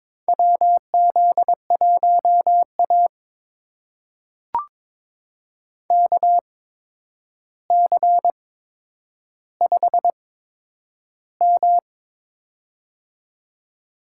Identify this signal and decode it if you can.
Morse code, heard as a keyed tone, about 22 wpm, spaced as for 9 wpm, 700 Hz. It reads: WZ1A KC5M
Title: USA Call Signs - 4 Characters - 8x Character Spacing - 22wpm